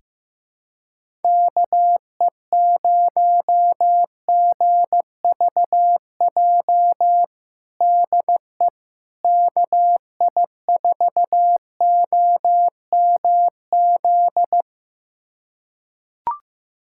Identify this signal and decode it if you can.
Morse code, heard as a keyed tone, about 15 wpm, 700 Hz. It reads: KE0GVJ DE KI4OMZ
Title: USA Callsign DE Callsign 15wpm